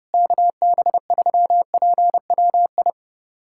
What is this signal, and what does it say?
Morse code, heard as a keyed tone, about 30 wpm, 700 Hz. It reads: KB3PWI
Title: USA Callsign DE Callsign 30wpm